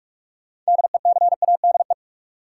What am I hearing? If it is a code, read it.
Morse code, heard as a keyed tone, about 41 wpm, 700 Hz. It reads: DECADE
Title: Sets of 5 Words - New Word Drill - 45wpm